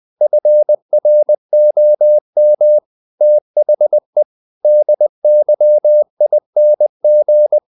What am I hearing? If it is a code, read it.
Morse code, heard as a keyed tone, about 20 wpm, 600 Hz. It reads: FROM THE DYING
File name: the old man and the sea / the old man and the sea part 1 20wpm